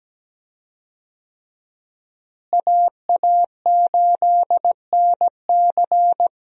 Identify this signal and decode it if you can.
Morse code, heard as a keyed tone, about 17 wpm, 700 Hz. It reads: AA8NC
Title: USA Call Signs - 5 Characters - Repeated 3x - 17wpm